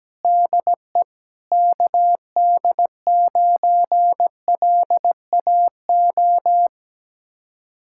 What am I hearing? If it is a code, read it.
Morse code, heard as a keyed tone, about 17 wpm, 700 Hz. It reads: DE KD9LAO
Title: Speed-Racing Common QSO Phrases Encore - 17wpm